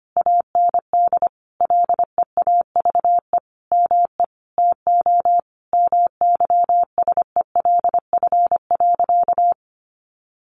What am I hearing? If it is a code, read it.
Morse code, heard as a keyed tone, about 25 wpm, 700 Hz. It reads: AND LEAVE ME TO MYSELF.
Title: Sentences from Top 1300 Words 25wpm